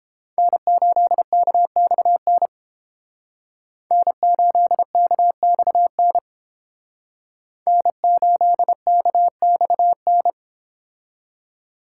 Morse code, heard as a keyed tone, about 31 wpm, 700 Hz. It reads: N8KXN N8KXN N8KXN